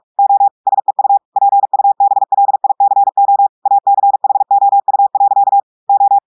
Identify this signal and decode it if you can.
Morse code, heard as a keyed tone, about 46 wpm, 800 Hz. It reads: O SEU PUBLICO ACHOU. O
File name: audios / portugais / portugais texte en clair 2 vitesse 45 mots minutes